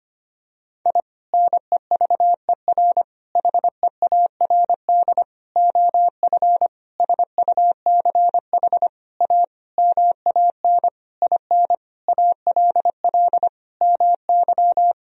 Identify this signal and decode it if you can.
Morse code, heard as a keyed tone, about 25 wpm, 700 Hz. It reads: I NEVER HEARD OF SUCH A MAN IN ALL MY
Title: Sentences from Top 1000 Words - Review Part 3 - 25wpm